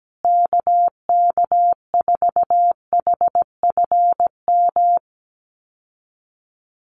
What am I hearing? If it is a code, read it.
Morse code, heard as a keyed tone, about 17 wpm, 700 Hz. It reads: KK4HFM
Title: USA Call Signs Repeated 3x 17wpm